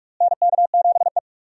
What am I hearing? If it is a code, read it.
Morse code, heard as a keyed tone, about 46 wpm, 700 Hz. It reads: NK7E
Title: USA Call Signs - 4 Characters - Repeated 2x - 45wpm